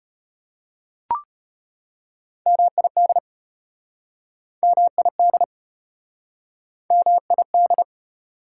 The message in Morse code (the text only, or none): MID MID MID